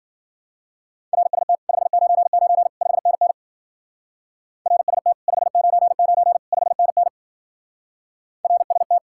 Morse code, heard as a keyed tone, about 50 wpm, 700 Hz. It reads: RST 599 5NN RST 599 5NN RST
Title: Speed-Racing Common QSO Phrases Encore - 40wpm